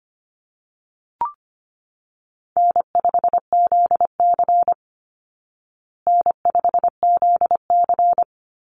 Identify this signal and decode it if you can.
Morse code, heard as a keyed tone, about 25 wpm, 700 Hz. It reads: N5ZC N5ZC
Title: Call Signs Repeated 3x - 25wpm